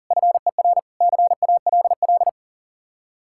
Morse code, heard as a keyed tone, about 50 wpm, 700 Hz. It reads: FER CALL
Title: Speed-Racing Common QSO Phrases Encore - 40wpm